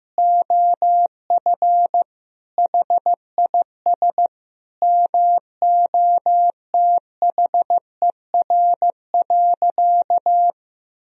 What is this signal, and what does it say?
Morse code, heard as a keyed tone, about 15 wpm, 700 Hz. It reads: OF HIS MOTHER.
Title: Sentences from Top 1000 Words - Review Part 4 - 15wpm